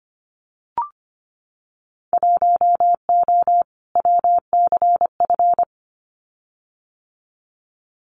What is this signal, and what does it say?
Morse code, heard as a keyed tone, about 25 wpm, 700 Hz. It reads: 1O WCF
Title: Winter Field Day Exchanges 25wpm